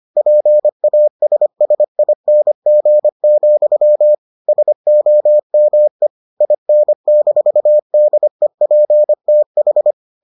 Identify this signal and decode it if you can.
Morse code, heard as a keyed tone, about 25 wpm, 600 Hz. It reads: PASSING, SOME IN-DEPTH